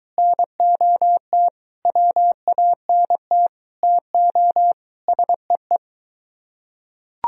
Morse code, heard as a keyed tone, about 21 wpm, 700 Hz. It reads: NOT WANT TO SEE